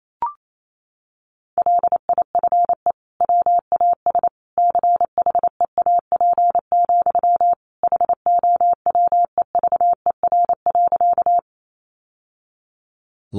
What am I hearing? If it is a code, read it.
Morse code, heard as a keyed tone, about 28 wpm, 700 Hz. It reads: LIFE WAS CHEAP, HOWEVER.
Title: Sentences from Top 1200 Words 28wpm